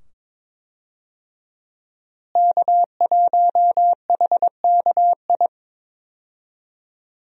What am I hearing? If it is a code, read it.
Morse code, heard as a keyed tone, about 22 wpm, 700 Hz. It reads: K1HKI
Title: USA Call Signs - 5 Characters - Repeated 2x - 22wpm